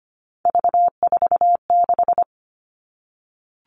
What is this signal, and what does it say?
Morse code, heard as a keyed tone, about 25 wpm, 700 Hz. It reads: V46
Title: All Cut Numbers - 3-Digit 25wpm